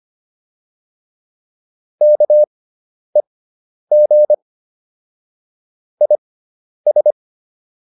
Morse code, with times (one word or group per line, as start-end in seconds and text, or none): 2.0-4.3 KEG
6.0-7.1 IS